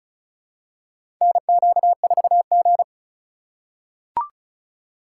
Morse code, heard as a keyed tone, about 36 wpm, 700 Hz. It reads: NQ4G
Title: USA Call Signs - 4 Characters - 2x Character Spacing - 35wpm